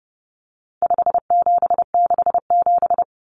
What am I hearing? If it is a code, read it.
Morse code, heard as a keyed tone, about 30 wpm, 700 Hz. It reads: 5767
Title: Common Cut Numbers - 4-Digit 30wpm